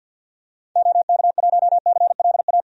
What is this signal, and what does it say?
Morse code, heard as a keyed tone, about 48 wpm, 700 Hz. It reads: OK1CLA
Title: DXCC-100 Call Signs - 6 Characters - Repeated 3x - 50wpm